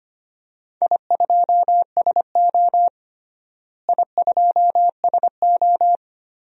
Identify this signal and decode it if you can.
Morse code, heard as a keyed tone, about 25 wpm, 700 Hz. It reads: I2SO I2SO
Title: DXCC-100 Call Signs - 4 Characters - Repeated 3x - 25wpm